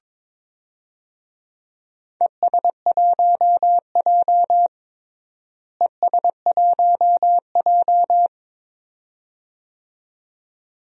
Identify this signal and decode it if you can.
Morse code, heard as a keyed tone, about 22 wpm, 700 Hz. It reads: ES1J ES1J
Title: DXCC-100 Call Signs - 4 Characters - Repeated 2x - 22wpm